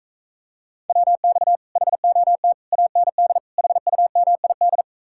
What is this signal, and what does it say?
Morse code, heard as a keyed tone, about 44 wpm, 700 Hz. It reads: WX HOT AND HUMID